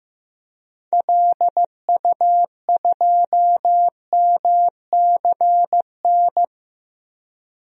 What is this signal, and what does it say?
Morse code, heard as a keyed tone, about 15 wpm, 700 Hz. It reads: LU2MCN
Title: DXCC-100 Call Signs - 6 Characters - 15wpm